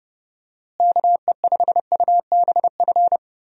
Morse code, heard as a keyed tone, about 30 wpm, 700 Hz. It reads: KE5UBF
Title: USA Call Signs - 6 Characters - Repeated 2x - 30wpm